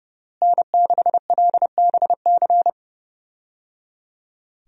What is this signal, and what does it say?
Morse code, heard as a keyed tone, about 30 wpm, 700 Hz. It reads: N6LBC
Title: USA Call Signs - 5 Characters - Repeated 3x - 30wpm